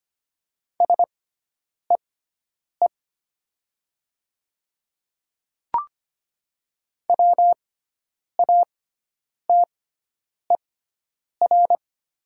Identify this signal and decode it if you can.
Morse code, heard as a keyed tone, about 25 wpm, spaced as for 9 wpm, 700 Hz. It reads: SEE WATER
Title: 6x Character Spacing - Top 300 Words - 25wpm